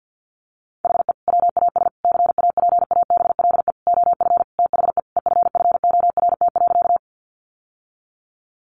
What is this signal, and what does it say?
Morse code, heard as a keyed tone, about 50 wpm, 700 Hz. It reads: HE WAS CAPABLE OF THE EFFORT.
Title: Sentences from Top 1200 Words 50wpm